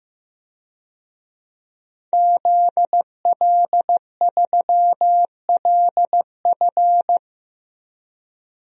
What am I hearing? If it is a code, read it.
Morse code, heard as a keyed tone, about 15 wpm, 700 Hz. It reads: ZL3LF
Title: DXCC-100 Call Signs - 5 Characters - 15wpm